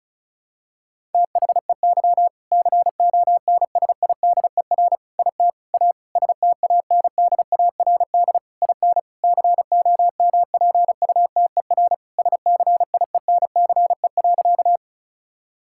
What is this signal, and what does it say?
Morse code, heard as a keyed tone, about 36 wpm, 700 Hz. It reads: THEY CONSIDER IT A STANDARD IN COMPUTER SCIENCE.
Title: Sentences from Top 1000 Words - Review Part 6 - 35wpm